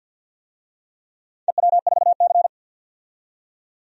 Morse code, heard as a keyed tone, about 46 wpm, 700 Hz. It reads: EW4C
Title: DXCC-100 Call Signs - 4 Characters - 50wpm